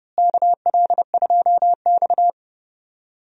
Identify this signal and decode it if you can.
Morse code, heard as a keyed tone, about 32 wpm, 700 Hz. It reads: KL2X
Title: USA Call Signs - 4 Characters - Repeated 3x - 30wpm